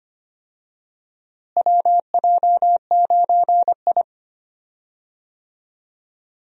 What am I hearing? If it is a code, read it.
Morse code, heard as a keyed tone, about 25 wpm, 700 Hz. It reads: WJ9I
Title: USA Call Signs - 4 Characters - Repeated 3x - 25wpm